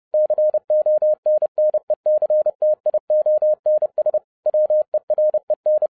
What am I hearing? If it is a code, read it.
Morse code, heard as a keyed tone, about 30 wpm, 610 Hz. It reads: CONNECTIONS WEREN